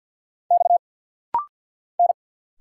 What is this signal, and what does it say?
Morse code, heard as a keyed tone, about 50 wpm, 700 Hz. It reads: X N
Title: Single Letters - Rapid-Fire 50wpm